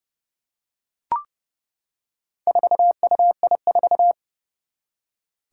Morse code, heard as a keyed tone, about 29 wpm, 700 Hz. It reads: 4UI4
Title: Four Challenging Letter-Number Combinations 30wpm